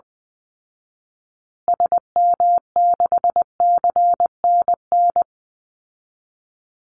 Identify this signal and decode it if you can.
Morse code, heard as a keyed tone, about 20 wpm, 700 Hz. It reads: SM6CNN
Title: Call Signs Repeated 3x - 20wpm